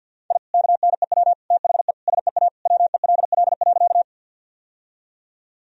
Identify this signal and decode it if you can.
Morse code, heard as a keyed tone, about 48 wpm, 700 Hz. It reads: I KNEW THE SEA WELL.
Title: Sentences from Top 1000 Words - Review Part 1 - 50wpm